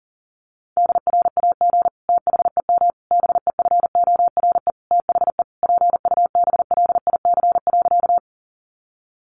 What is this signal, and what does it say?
Morse code, heard as a keyed tone, about 40 wpm, 700 Hz. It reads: DRAG THEM BEFORE THE PUBLIC.
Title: Sentences from Top 1500 Words 40wpm